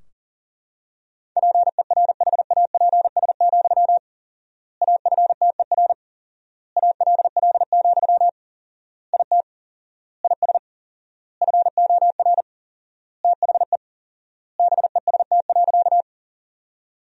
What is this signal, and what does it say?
Morse code, heard as a keyed tone, about 40 wpm, 700 Hz. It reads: PERHAPS, AFTER ALL, IT IS FOR THE BEST.
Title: 4x Word-Spacing - Sentences from Top 500 Words - 40wpm